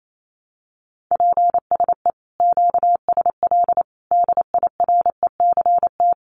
Morse code, heard as a keyed tone, about 28 wpm, 700 Hz. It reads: PSE QSL DIRECT
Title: Common QSO Phrases 28wpm